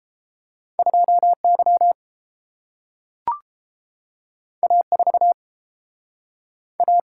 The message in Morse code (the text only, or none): 2Y A4 A